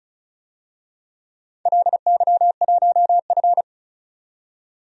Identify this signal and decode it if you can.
Morse code, heard as a keyed tone, about 35 wpm, 700 Hz. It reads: LY1F
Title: DXCC-100 Call Signs - 4 Characters - 35wpm